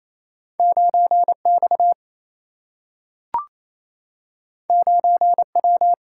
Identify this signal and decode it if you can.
Morse code, heard as a keyed tone, about 28 wpm, 700 Hz. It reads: 9X 9W
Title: Single-Letter Single-Number 28wpm